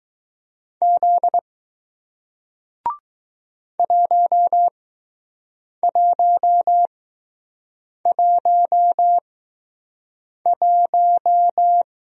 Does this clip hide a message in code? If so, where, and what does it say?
Yes: Morse code from 0.8 to 11.8 s, heard as a keyed tone, about 20 wpm, spaced as for 6 wpm, 700 Hz. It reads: Z 1111